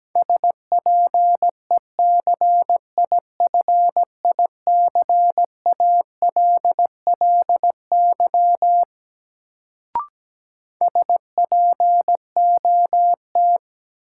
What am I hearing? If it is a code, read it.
Morse code, heard as a keyed tone, about 17 wpm, 700 Hz. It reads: SPECIFICALLY SPOT
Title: Sentences from Top 1100 Words - New Word Drill Alt - 17wpm